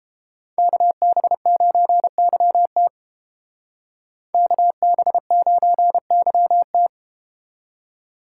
Morse code, heard as a keyed tone, about 32 wpm, 700 Hz. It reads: KB9YT KB9YT